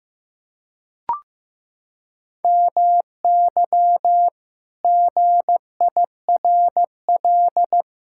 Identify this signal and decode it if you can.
Morse code, heard as a keyed tone, about 16 wpm, 700 Hz. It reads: MY GIRL
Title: Pangram Challenge v2 - 15wpm